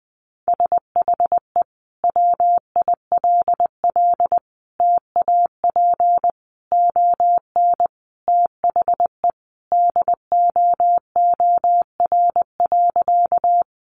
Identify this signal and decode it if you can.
Morse code, heard as a keyed tone, about 20 wpm, 700 Hz. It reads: SHE WILL TAP ON THE DOOR.
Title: Sentences from Top 1400 Words 20wpm